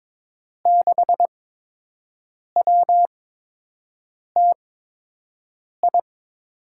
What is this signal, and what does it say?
Morse code, heard as a keyed tone, about 22 wpm, spaced as for 6 wpm, 700 Hz. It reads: 6WTI
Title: USA Call Signs - 6 Characters - 8x Character Spacing - 22wpm